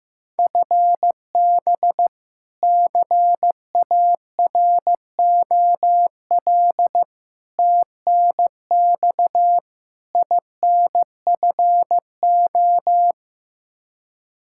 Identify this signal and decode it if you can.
Morse code, heard as a keyed tone, about 16 wpm, 700 Hz. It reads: FB CAROL TNX INFO
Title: K1USN SST QSO Practice 15wpm